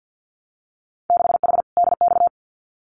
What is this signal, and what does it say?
Morse code, heard as a keyed tone, about 50 wpm, 700 Hz. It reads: 6H DX
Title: Winter Field Day Exchanges Encore 50wpm